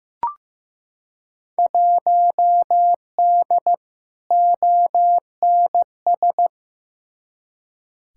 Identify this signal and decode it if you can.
Morse code, heard as a keyed tone, about 15 wpm, 700 Hz. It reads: 1D ONS
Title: Field Day Exchanges - 15wpm